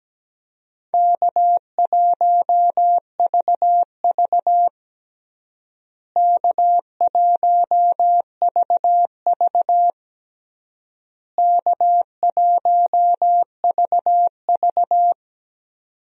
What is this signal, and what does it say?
Morse code, heard as a keyed tone, about 17 wpm, 700 Hz. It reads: K1VV K1VV K1VV
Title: USA Call Signs - 4 Characters - Repeated 3x - 17wpm